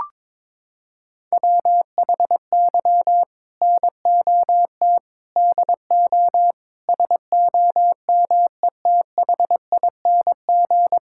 Morse code, heard as a keyed tone, about 22 wpm, 700 Hz. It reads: WHY NOT DO SOMETHING
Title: Sentences from Top 1000 Words - Review Part 6 - 22wpm